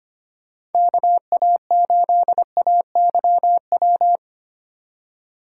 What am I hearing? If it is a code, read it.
Morse code, heard as a keyed tone, about 31 wpm, 700 Hz. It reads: KA8AYW
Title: Speed Racing - USA 2x3 Call Signs 25wpm